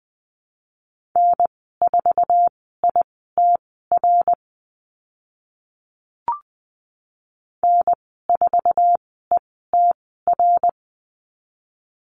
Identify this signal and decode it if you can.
Morse code, heard as a keyed tone, about 20 wpm, spaced as for 14 wpm, 700 Hz. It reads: N4ITR N4ETR
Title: 082 Lesson 14g - Number 2 - Call Signs 20wpm